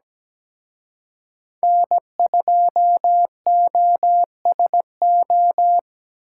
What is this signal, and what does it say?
Morse code, heard as a keyed tone, about 20 wpm, 700 Hz. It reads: N2OSO
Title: Speed Racing - USA 1x3 Call Signs 17wpm